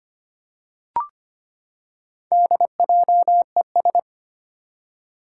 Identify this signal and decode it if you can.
Morse code, heard as a keyed tone, about 25 wpm, 700 Hz. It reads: DJES
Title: Four Challenging Letter-Number Combinations 25wpm